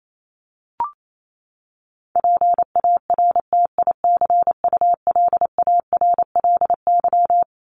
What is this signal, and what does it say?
Morse code, heard as a keyed tone, about 28 wpm, 700 Hz. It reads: PARTICULARLY
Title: LY Suffix - Top 100 Words with Base Words 28wpm